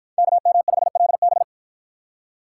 Morse code, heard as a keyed tone, about 53 wpm, 700 Hz. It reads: KM5LB